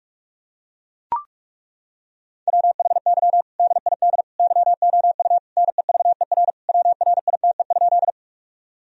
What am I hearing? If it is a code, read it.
Morse code, heard as a keyed tone, about 41 wpm, 700 Hz. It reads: WHY DID YOU NEVER WRITE?
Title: Sentences from Top 1000 Words - Review Part 1 - 45wpm